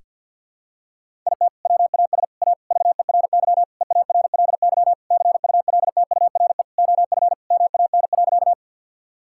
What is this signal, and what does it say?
Morse code, heard as a keyed tone, about 52 wpm, 700 Hz. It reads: IT WAS A VERY EARLY CULTURE OF MAN.
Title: Sentences from Top 1000 Words - Review Part 6 - 50wpm